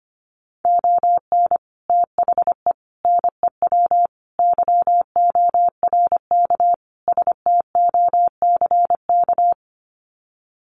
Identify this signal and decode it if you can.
Morse code, heard as a keyed tone, about 25 wpm, 700 Hz. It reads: ON THE NEW YORK STOCK
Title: Sets of 5 Words Encore 25wpm